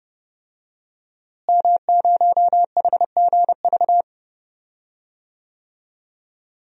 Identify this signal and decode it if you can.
Morse code, heard as a keyed tone, about 30 wpm, 700 Hz. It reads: M0HGV
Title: DXCC-100 Call Signs - 5 Characters - Repeated 3x - 30wpm